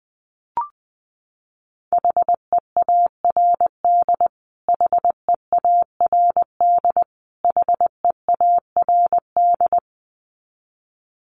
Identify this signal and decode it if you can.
Morse code, heard as a keyed tone, about 20 wpm, 700 Hz. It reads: HEARD HEARD HEARD
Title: Top 500 Words Repeated 20wpm